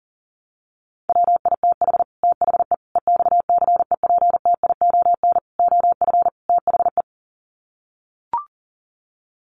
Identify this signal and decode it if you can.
Morse code, heard as a keyed tone, about 40 wpm, 700 Hz. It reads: WITH THE EXCEPTION OF THE
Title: Sets of 5 Words Encore 40wpm